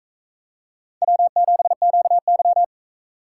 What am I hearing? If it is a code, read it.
Morse code, heard as a keyed tone, about 40 wpm, 700 Hz. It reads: W7QY